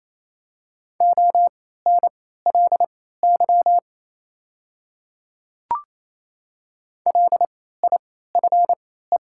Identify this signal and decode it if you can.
Morse code, heard as a keyed tone, about 28 wpm, spaced as for 16 wpm, 700 Hz. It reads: ONLY LIFE
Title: 3x Character Spacing - Top 300 Words - 28wpm